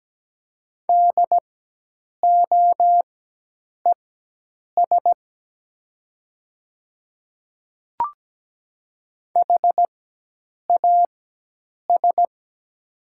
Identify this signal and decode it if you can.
Morse code, heard as a keyed tone, about 17 wpm, spaced as for 8 wpm, 700 Hz. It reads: DOES HAS
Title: 4x Character Spacing - Top 200 Words - 17wpm